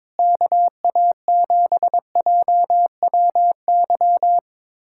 Morse code, heard as a keyed tone, about 22 wpm, 700 Hz. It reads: KA7JWY
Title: USA Callsign DE Callsign - 6 Characters - 22wpm